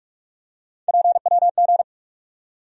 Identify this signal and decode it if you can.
Morse code, heard as a keyed tone, about 43 wpm, 700 Hz. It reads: PWG
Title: USA Call Sign Postfixes - 3 Letters 45wpm